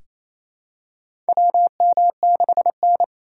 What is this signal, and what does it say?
Morse code, heard as a keyed tone, about 27 wpm, 700 Hz. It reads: WM6N
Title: USA Call Signs - 4 Characters - 4x Character Spacing - 28wpm